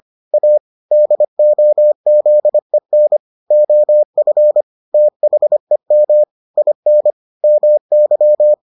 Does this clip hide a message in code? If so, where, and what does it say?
Yes: Morse code from 0.3 to 8.5 s, heard as a keyed tone, about 25 wpm, 600 Hz. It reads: A DOZEN OF THEM IN MY